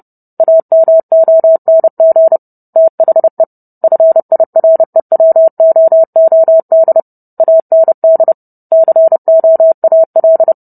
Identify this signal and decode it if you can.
Morse code, heard as a keyed tone, about 30 wpm, 650 Hz. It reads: AMONG THE FIREWOOD AND COAL